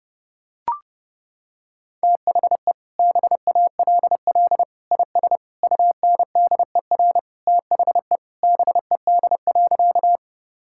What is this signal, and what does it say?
Morse code, heard as a keyed tone, about 30 wpm, 700 Hz. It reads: THE BALL IS UNDER THE BED.
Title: Sentences from Top 1000 Words - Review Part 6 - 30wpm